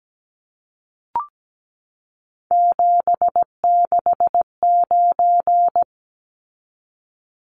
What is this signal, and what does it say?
Morse code, heard as a keyed tone, about 17 wpm, 700 Hz. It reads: 769
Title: Numbers - 3-Digit 17wpm